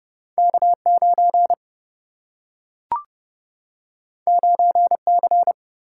Morse code, heard as a keyed tone, about 29 wpm, 700 Hz. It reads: K 9 9 C
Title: Single-Letter Single-Number 30wpm